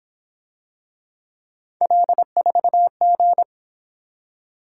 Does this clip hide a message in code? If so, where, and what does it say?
Yes: Morse code from 1.8 to 3.4 s, heard as a keyed tone, about 28 wpm, 700 Hz. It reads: L4G